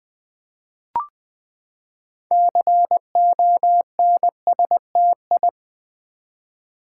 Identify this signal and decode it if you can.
Morse code, heard as a keyed tone, about 20 wpm, 700 Hz. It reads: CONSTI
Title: Build A Word - 6 Letters At A Time - 12-Letter Words - 20wpm